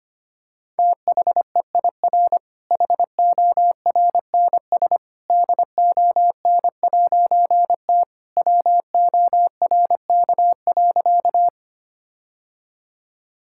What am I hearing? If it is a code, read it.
Morse code, heard as a keyed tone, about 25 wpm, 700 Hz. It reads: THEIR HORNS DON'T WORK.
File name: practice / 25wpm / All-Age Appropriate Jokes - 25wpm